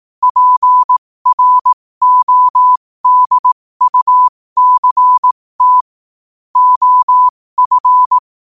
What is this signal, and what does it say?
Morse code, heard as a keyed tone, about 18 wpm, spaced as for 15 wpm, 1000 Hz. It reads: PRODUCT OF
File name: qotdfiles / 15wpm